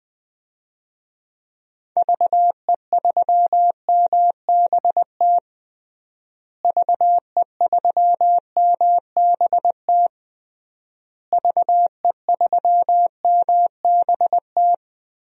Morse code, heard as a keyed tone, about 21 wpm, 700 Hz. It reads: VE3MBT VE3MBT VE3MBT
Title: DXCC-100 Call Signs - 6 Characters - Repeated 3x - 20wpm